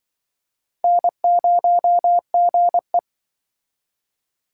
Morse code, heard as a keyed tone, about 26 wpm, 700 Hz. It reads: N0GE